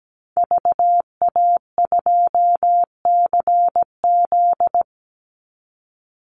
Morse code, heard as a keyed tone, about 17 wpm, 700 Hz. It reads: VA2CZ
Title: Call Signs Repeated 3x - 17wpm